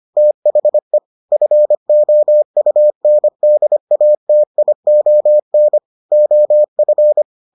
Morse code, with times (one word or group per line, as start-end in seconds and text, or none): 0.2-1.0 THE
1.3-5.8 FOUNDATION
6.1-7.2 OF